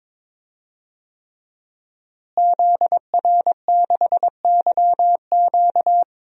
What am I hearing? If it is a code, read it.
Morse code, heard as a keyed tone, about 22 wpm, 700 Hz. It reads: ZR6YQ
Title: DXCC-100 Call Signs - 5 Characters - Repeated 3x - 22wpm